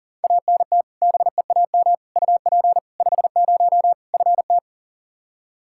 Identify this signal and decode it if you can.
Morse code, heard as a keyed tone, about 40 wpm, 700 Hz. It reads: ANT BEAM UP 50 FT
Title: Practice QSO - Exchange 2 - Info - 40wpm